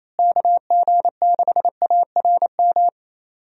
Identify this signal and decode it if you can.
Morse code, heard as a keyed tone, about 28 wpm, 700 Hz. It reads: KG6ARM
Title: USA Call Signs - 6 Characters - Repeated 2x - 28wpm